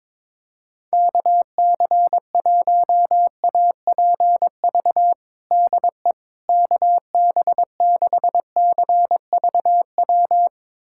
Morse code, heard as a keyed tone, about 21 wpm, 700 Hz. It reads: KC1APV DE KB6CVW
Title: USA Callsign DE Callsign 22wpm